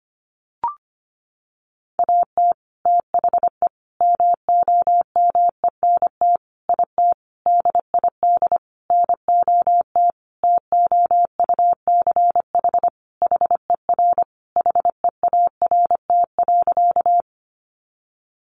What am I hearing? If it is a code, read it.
Morse code, heard as a keyed tone, about 25 wpm, 700 Hz. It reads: AT THE MOMENT IT DID NOT TOUCH HER HEART.
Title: Sentences from Top 1200 Words 25wpm